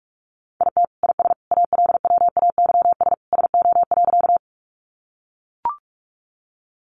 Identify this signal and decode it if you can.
Morse code, heard as a keyed tone, about 44 wpm, 700 Hz. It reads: IT IS ALWAYS SO.
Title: Sentences from Top 300 Words 45wpm